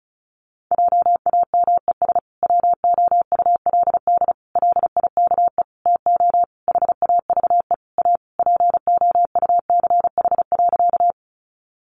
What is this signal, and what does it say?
Morse code, heard as a keyed tone, about 34 wpm, 700 Hz. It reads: JAMES WOULD LIKE TO HAVE A POUCH.
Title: Easy Phrases 35wpm